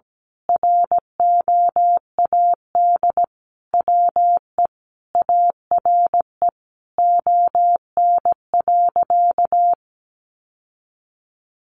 Morse code, heard as a keyed tone, about 17 wpm, 700 Hz. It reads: ROAD WE ARE ON.
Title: Sentences from Top 1100 Words 17wpm